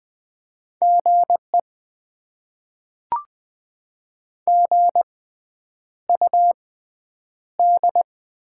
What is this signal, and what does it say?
Morse code, heard as a keyed tone, about 20 wpm, spaced as for 9 wpm, 700 Hz. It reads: GE GUD
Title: Top 20 QSO Elements - 6x Character Spacing - 20wpm